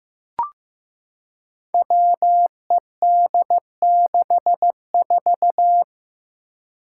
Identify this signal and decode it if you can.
Morse code, heard as a keyed tone, about 15 wpm, 700 Hz. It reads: WED64
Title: Five Challenging Letter-Number Combinations 15wpm